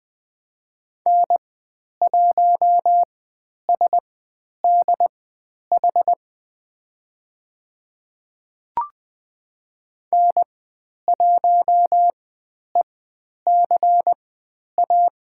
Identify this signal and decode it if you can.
Morse code, heard as a keyed tone, about 20 wpm, spaced as for 10 wpm, 700 Hz. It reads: N1SDH N1ECA
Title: 102 Lesson 17g - Letter u - Call Signs 20wpm